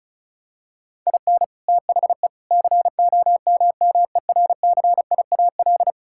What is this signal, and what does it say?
Morse code, heard as a keyed tone, about 36 wpm, 700 Hz. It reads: IN THE COMMERCIAL
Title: Sentences from Top 1000 Words - Review Part 5 - 35wpm